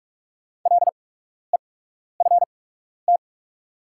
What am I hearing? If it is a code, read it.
Morse code, heard as a keyed tone, about 45 wpm, spaced as for 12 wpm, 700 Hz. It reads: LEFT
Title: 8x Character Spacing - Top 300 Words - 45wpm